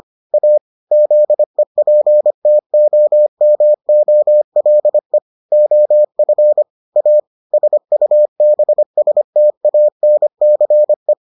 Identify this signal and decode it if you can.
Morse code, heard as a keyed tone, about 25 wpm, 600 Hz. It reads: A ZEPTOMOLE OF A SUBSTANCE